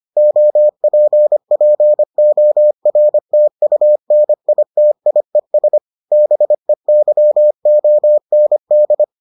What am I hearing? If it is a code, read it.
Morse code, heard as a keyed tone, about 25 wpm, 600 Hz. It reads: OPPORTUNITIES BEYOND